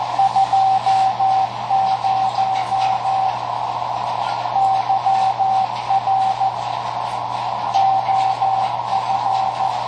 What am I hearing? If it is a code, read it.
Morse code, heard as a keyed tone, about 14 wpm, 770 Hz, with noise. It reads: 20 YR OF
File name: tracks / KA9SZX 3W Macomb IL